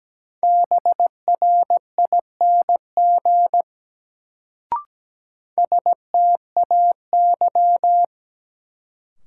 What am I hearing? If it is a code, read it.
Morse code, heard as a keyed tone, about 17 wpm, 700 Hz. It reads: BRING STAY
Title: Sentences from Top 500 Words - New Word Drill - 17wpm